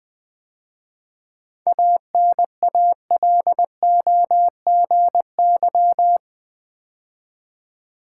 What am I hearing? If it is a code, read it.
Morse code, heard as a keyed tone, about 20 wpm, 700 Hz. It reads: ANALOGY
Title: 7 Letter Words - First 3 Letters Given - 20wpm